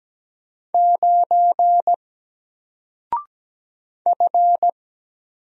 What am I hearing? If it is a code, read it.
Morse code, heard as a keyed tone, about 17 wpm, 700 Hz. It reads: 9 F
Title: Single Character (Letter:Number:Punctuation:Prosign) 17wpm